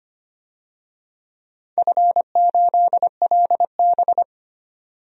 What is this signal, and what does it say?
Morse code, heard as a keyed tone, about 25 wpm, 700 Hz. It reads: F8LB
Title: DXCC-100 Call Signs - 4 Characters - 25wpm